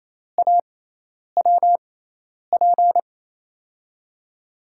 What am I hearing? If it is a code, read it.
Morse code, heard as a keyed tone, about 28 wpm, spaced as for 10 wpm, 700 Hz. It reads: AWP
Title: USA Call Signs - 5 Characters - 6x Character Spacing - 28wpm